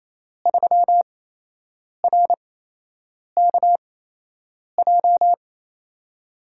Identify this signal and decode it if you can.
Morse code, heard as a keyed tone, about 28 wpm, spaced as for 8 wpm, 700 Hz. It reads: 3RKJ
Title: USA Call Signs - 5 Characters - 8x Character Spacing - 28wpm